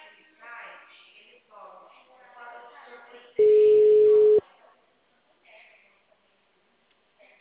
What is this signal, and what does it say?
Ringback tone (Europe, 425 Hz), with speech.